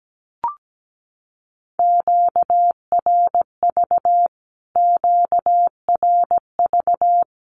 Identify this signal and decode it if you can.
Morse code, heard as a keyed tone, about 17 wpm, 700 Hz. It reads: QRV QRV
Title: Q-Codes Repeated 3x 17wpm